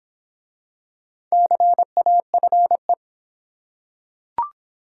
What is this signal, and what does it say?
Morse code, heard as a keyed tone, about 26 wpm, 700 Hz. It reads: CAFE